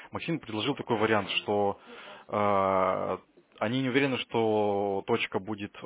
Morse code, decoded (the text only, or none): none